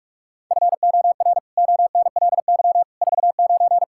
Morse code, heard as a keyed tone, about 46 wpm, 700 Hz. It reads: FOR ONLY 49
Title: Pangram Challenge v2 - 45wpm